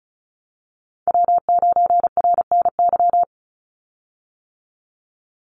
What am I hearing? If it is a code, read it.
Morse code, heard as a keyed tone, about 36 wpm, 700 Hz. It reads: W9RNY
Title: Call Signs - 35wpm